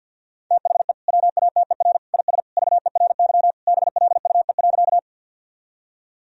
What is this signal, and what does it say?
Morse code, heard as a keyed tone, about 50 wpm, 700 Hz. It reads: THE WATER IS VERY BLUE.
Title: Sentences from Top 1000 Words - Review Part 3 - 50wpm